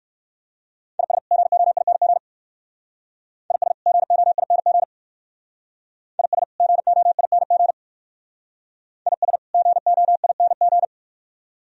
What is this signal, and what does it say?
Morse code, heard as a keyed tone, about 56 wpm, 690 Hz. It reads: IS GOING IS GOING IS GOING IS GOING
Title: Speed-Racing - Sets of 2 Words 45wpm